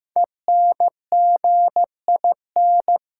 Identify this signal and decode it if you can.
Morse code, heard as a keyed tone, about 16 wpm, 700 Hz. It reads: ENGIN
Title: Sentences from Top 1500 Words - New Word Drill Alt - 15wpm